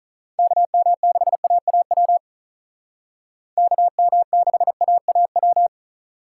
Morse code, heard as a keyed tone, about 46 wpm, 700 Hz. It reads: KM6AAW KM6AAW